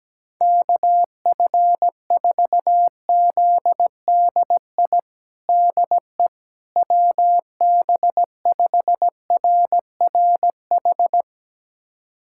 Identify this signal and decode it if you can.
Morse code, heard as a keyed tone, about 17 wpm, 700 Hz. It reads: KF4ZDI DE WB5RRH
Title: USA Callsign DE Callsign - 6 Characters - 17wpm